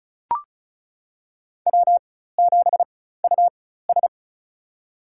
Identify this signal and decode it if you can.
Morse code, heard as a keyed tone, about 34 wpm, spaced as for 16 wpm, 700 Hz. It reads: W7US
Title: USA Call Signs - 4 Characters - 4x Character Spacing - 35wpm